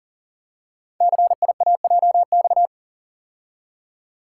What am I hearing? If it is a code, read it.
Morse code, heard as a keyed tone, about 42 wpm, 700 Hz. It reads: CIAJX
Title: Code Groups - 5 Letters - 40wpm